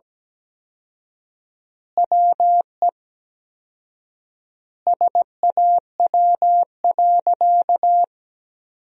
Morse code, heard as a keyed tone, about 17 wpm, 700 Hz. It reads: WE SAW.
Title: 4x Word-Spacing - Sentences from Top 300 Words - 17wpm